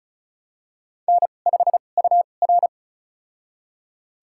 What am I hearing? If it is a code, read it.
Morse code, heard as a keyed tone, about 34 wpm, spaced as for 25 wpm, 700 Hz. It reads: N5UR
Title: USA Call Signs - 4 Characters - 2x Character Spacing - 35wpm